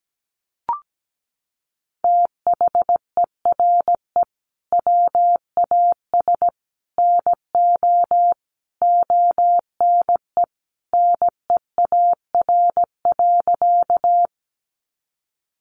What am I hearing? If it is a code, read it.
Morse code, heard as a keyed tone, about 17 wpm, 700 Hz. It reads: THERE WAS NO ONE NEAR.
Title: Sentences from Top 200 Words 17wpm